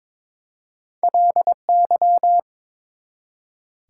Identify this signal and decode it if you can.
Morse code, heard as a keyed tone, about 22 wpm, 700 Hz. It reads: LY
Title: Prefixes and Suffixes 22wpm